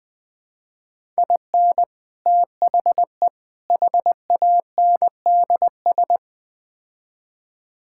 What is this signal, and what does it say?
Morse code, heard as a keyed tone, about 20 wpm, 700 Hz. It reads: IN THE HANDS
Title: Build A Phrase - 4-Word Phrases - 20wpm